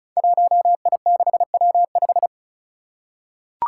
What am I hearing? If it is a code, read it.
Morse code, heard as a keyed tone, about 36 wpm, 700 Hz. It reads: 1I6W5
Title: Five Challenging Letter-Number Combinations 35wpm